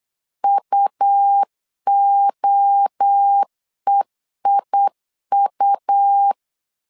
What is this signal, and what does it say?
Morse code, heard as a keyed tone, about 8 wpm, 800 Hz. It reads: UOEIU
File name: CW / DET / 1 3 1500C S B 8wpm 4141